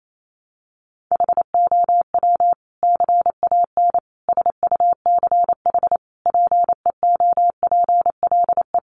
Morse code, heard as a keyed tone, about 28 wpm, 700 Hz. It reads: HOW CAN SUCH PEOPLE
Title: Sentences from Top 1100 Words 28wpm